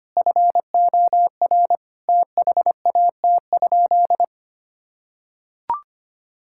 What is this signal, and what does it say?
Morse code, heard as a keyed tone, about 25 wpm, 700 Hz. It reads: FOR THAT?
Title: Sentences from Top 1000 Words - Review Part 6 - 25wpm